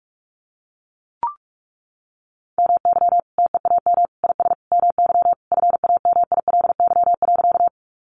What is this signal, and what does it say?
Morse code, heard as a keyed tone, about 44 wpm, 700 Hz. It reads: MY TEAM IS MY FAMILY.
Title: Sentences from Top 1500 Words 45wpm